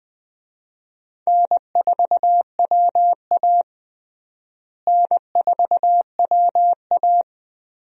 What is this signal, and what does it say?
Morse code, heard as a keyed tone, about 20 wpm, 700 Hz. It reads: N4WA N4WA
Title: USA Call Signs - 4 Characters - Repeated 3x - 20wpm